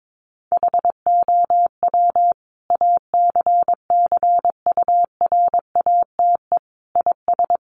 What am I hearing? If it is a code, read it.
Morse code, heard as a keyed tone, about 22 wpm, 700 Hz. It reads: HOW ACCURATE IS
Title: Sentences from Top 1200 Words 22wpm